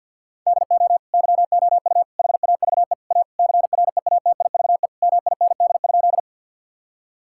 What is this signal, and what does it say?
Morse code, heard as a keyed tone, about 50 wpm, 700 Hz. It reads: DO YOU HAVE A CREATIVE MIND?
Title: Sentences from Top 1000 Words - Review Part 5 - 50wpm